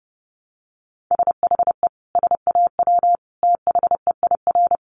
Morse code, heard as a keyed tone, about 30 wpm, 700 Hz. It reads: SHE SAW THEIR
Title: Sentences from Top 1200 Words 30wpm